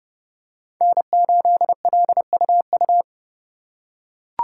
Morse code, heard as a keyed tone, about 31 wpm, 700 Hz. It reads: N8LUU